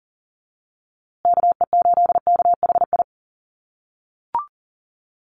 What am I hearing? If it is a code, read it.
Morse code, heard as a keyed tone, about 39 wpm, 700 Hz. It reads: KE8KHI
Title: USA Call Signs Repeated 2x 40wpm